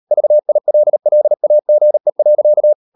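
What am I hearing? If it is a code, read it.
Morse code, heard as a keyed tone, about 38 wpm, 600 Hz. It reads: VILLAGE.
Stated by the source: Morse code practice at 35–39 wpm sending English text of Ken Follett's novel Never, from CW med Gnister.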